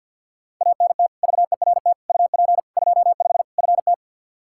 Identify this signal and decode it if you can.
Morse code, heard as a keyed tone, about 45 wpm, 700 Hz. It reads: ANT VERT UP 25 FT
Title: K1USN SST QSO Practice 50wpm